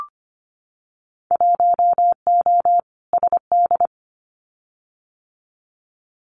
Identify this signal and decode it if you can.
Morse code, heard as a keyed tone, about 25 wpm, 700 Hz. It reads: 1O SD
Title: Winter Field Day Exchanges Encore 25wpm